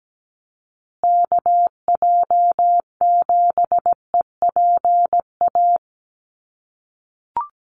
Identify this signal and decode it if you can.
Morse code, heard as a keyed tone, about 17 wpm, 700 Hz. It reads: KJ7EPA
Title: USA Call Signs Repeated 2x 17wpm